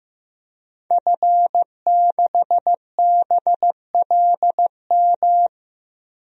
Morse code, heard as a keyed tone, about 15 wpm, 700 Hz. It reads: F6BLM
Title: DXCC-100 Call Signs - 5 Characters - 15wpm